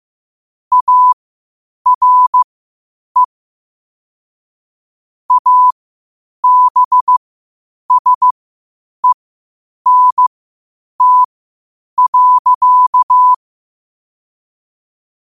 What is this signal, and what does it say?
Morse code, heard as a keyed tone, about 15 wpm, spaced as for 8 wpm, 1000 Hz. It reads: ARE ABSENT.
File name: qotdfiles / 8wpm